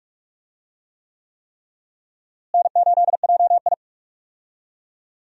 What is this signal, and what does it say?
Morse code, heard as a keyed tone, about 46 wpm, 700 Hz. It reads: N8JI
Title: USA Call Signs - 4 Characters - Repeated 3x - 45wpm